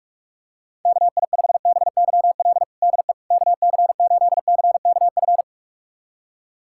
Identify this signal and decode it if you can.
Morse code, heard as a keyed tone, about 47 wpm, 700 Hz. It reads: KI5BYL DE KC8CKF
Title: USA Callsign DE Callsign - 6 Characters - 45wpm